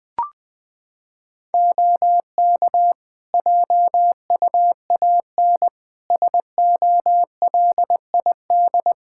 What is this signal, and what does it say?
Morse code, heard as a keyed tone, about 21 wpm, 700 Hz. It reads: OK JUAN SOLID
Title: K1USN SST QSO Practice 20wpm